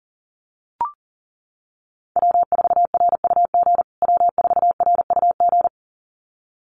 Morse code, heard as a keyed tone, about 40 wpm, 700 Hz. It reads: W4RUG W4RUG
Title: USA Call Signs Repeated 2x 40wpm